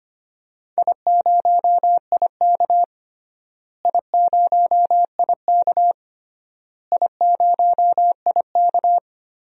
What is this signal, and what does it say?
Morse code, heard as a keyed tone, about 25 wpm, 700 Hz. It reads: I0IK I0IK I0IK
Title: DXCC-100 Call Signs - 4 Characters - Repeated 3x - 25wpm